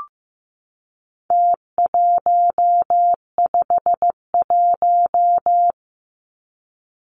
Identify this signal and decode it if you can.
Morse code, heard as a keyed tone, about 15 wpm, 700 Hz. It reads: T151
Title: Numbers - Contest Style - 4-Digit 15wpm